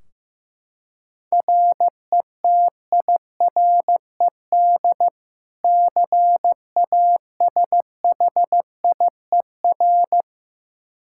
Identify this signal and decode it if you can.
Morse code, heard as a keyed tone, about 15 wpm, 700 Hz. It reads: RETIRED CASHIER
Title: K1USN SST QSO Practice 15wpm